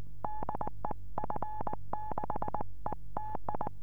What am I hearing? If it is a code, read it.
Morse code, heard as a keyed tone, about 19 wpm, 850 Hz. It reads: DE F6ETI
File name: 432 eme / f6eti-dl9kr 432